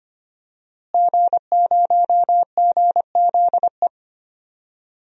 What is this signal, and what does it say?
Morse code, heard as a keyed tone, about 25 wpm, 700 Hz. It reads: G0GZE
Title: DXCC-100 Call Signs - 5 Characters - Repeated 3x - 25wpm